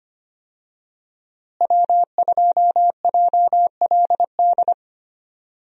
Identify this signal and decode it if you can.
Morse code, heard as a keyed tone, about 25 wpm, 700 Hz. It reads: W2JLD
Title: USA Call Signs - 5 Characters - Repeated 2x - 25wpm